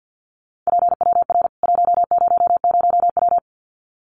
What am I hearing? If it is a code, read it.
Morse code, heard as a keyed tone, about 50 wpm, 700 Hz. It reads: PWR 100W
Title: Common QSO Phrases 50wpm